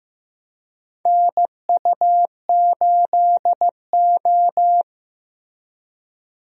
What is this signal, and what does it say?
Morse code, heard as a keyed tone, about 15 wpm, 700 Hz. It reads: NU8O
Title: USA Call Signs - 4 Characters - Repeated 3x - 15wpm